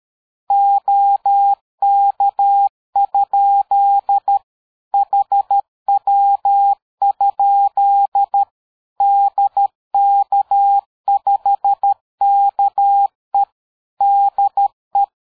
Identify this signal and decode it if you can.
Morse code, heard as a keyed tone, about 13 wpm, 800 Hz. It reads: OK? HW? DK5KE DE
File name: dk5ke / QSO 60BpM